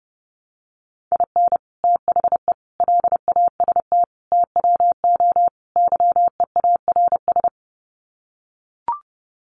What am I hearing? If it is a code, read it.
Morse code, heard as a keyed tone, about 30 wpm, 700 Hz. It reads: IN THE LAST TWO YEARS
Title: Sets of 5 Words 30wpm